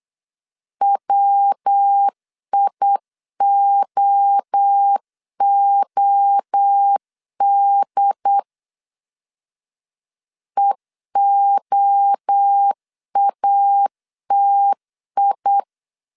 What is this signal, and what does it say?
Morse code, heard as a keyed tone, about 8 wpm, 800 Hz. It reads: WIOOD EOATI